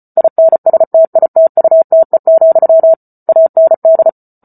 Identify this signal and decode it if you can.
Morse code, heard as a keyed tone, about 34 wpm, 650 Hz. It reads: INSTITUTE, AND